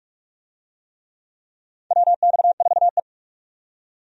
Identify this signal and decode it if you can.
Morse code, heard as a keyed tone, about 44 wpm, 700 Hz. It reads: WX4E
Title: USA Call Signs - 4 Characters - 45wpm